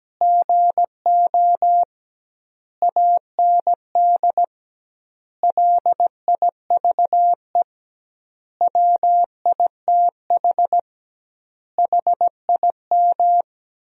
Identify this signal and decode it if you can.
Morse code, heard as a keyed tone, about 17 wpm, 700 Hz. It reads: GO AND LIVE WITH HIM
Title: 2x Word-Spacing - Sentences from Top 500 Words - 17wpm